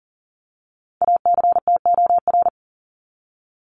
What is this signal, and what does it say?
Morse code, heard as a keyed tone, about 40 wpm, 700 Hz. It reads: ACTOR
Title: Top 1251-1500 Words 40wpm